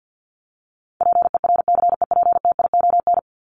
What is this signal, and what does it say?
Morse code, heard as a keyed tone, about 50 wpm, 700 Hz. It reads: PERCEPTION
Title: Top 1751-2000 Words 50wpm